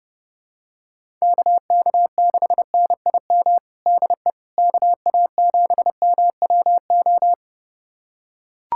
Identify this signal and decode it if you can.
Morse code, heard as a keyed tone, about 32 wpm, 700 Hz. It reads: KK6NIM DE KA7MWO
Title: USA Callsign DE Callsign - 6 Characters - 30wpm